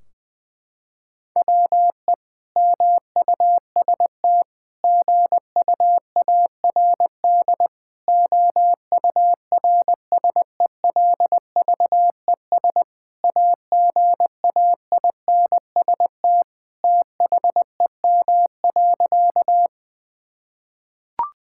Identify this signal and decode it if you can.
Morse code, heard as a keyed tone, about 20 wpm, 700 Hz. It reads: WE MUST GUARD OURSELVES AGAINST THEM.
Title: Sentences from Top 1600 Words 20wpm